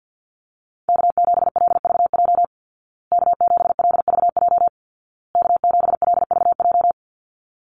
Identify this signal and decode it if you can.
Morse code, heard as a keyed tone, about 50 wpm, 700 Hz. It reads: K7LVJ K7LVJ K7LVJ
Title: Call Signs Repeated 2x - 50wpm